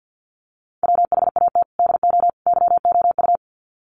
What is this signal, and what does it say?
Morse code, heard as a keyed tone, about 50 wpm, 700 Hz. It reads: WHAT DO YOU
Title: Sets of 3 Words 50wpm